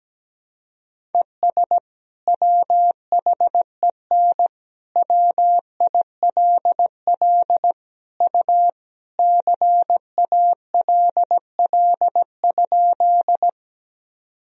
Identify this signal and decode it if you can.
Morse code, heard as a keyed tone, about 17 wpm, 700 Hz. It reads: ES WHEN WILL U CALL?
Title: Sentences from Top 100 Words and Abbreviations 17wpm